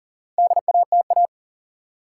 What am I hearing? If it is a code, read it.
Morse code, heard as a keyed tone, about 40 wpm, 700 Hz. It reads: DATA
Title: Sentences from Top 1000 Words - Review Part 3 - New Word Drill - 40wpm